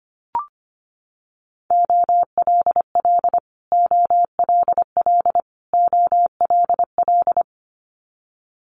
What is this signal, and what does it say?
Morse code, heard as a keyed tone, about 25 wpm, 700 Hz. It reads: OLL OLL OLL
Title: Multi-letter Phonemes Repeated 3x 25wpm